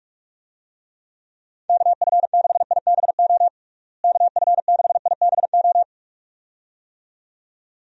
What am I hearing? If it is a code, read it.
Morse code, heard as a keyed tone, about 46 wpm, 700 Hz. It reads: KF6IBO KF6IBO
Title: USA Call Signs - 6 Characters - Repeated 2x - 45wpm